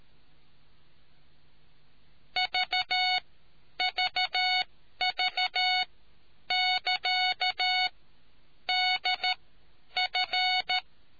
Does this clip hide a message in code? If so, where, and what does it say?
Yes: Morse code from 2.4 to 10.8 s, heard as a keyed tone, about 13 wpm, spaced as for 9 wpm, 740 Hz. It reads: VVV<KA> DF